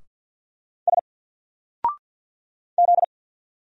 Morse code, heard as a keyed tone, about 50 wpm, 700 Hz. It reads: S Z
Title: Single Letter-Number 50wpm